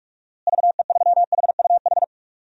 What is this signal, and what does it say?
Morse code, heard as a keyed tone, about 46 wpm, 700 Hz. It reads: VE3HUH
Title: DXCC-100 Call Signs - 6 Characters - Repeated 3x - 45wpm